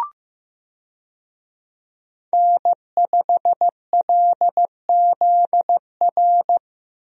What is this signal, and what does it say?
Morse code, heard as a keyed tone, about 16 wpm, 700 Hz. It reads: N5LZR
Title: USA Call Signs - 5 Characters - Repeated 3x - 15wpm